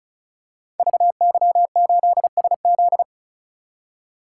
Morse code, heard as a keyed tone, about 33 wpm, 700 Hz. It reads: VY8SZ
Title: Code Groups - 5 Letter:Numbers - 35wpm